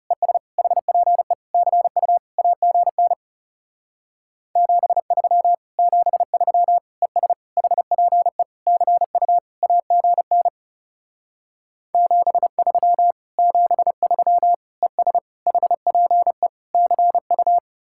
Morse code, heard as a keyed tone, about 37 wpm, 700 Hz. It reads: ES HPE CU AGN 73 73 ES HPE CU AGN 73 73 ES HPE CU
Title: Speed-Racing Common QSO Phrases Encore - 30wpm